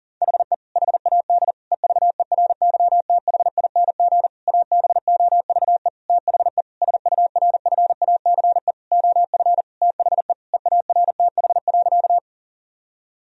Morse code, heard as a keyed tone, about 40 wpm, 700 Hz. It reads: HE HAD EVERYTHING ABOVE THE SURFACE OF THE EARTH.
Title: Sentences from Top 1000 Words - Review Part 3 - 40wpm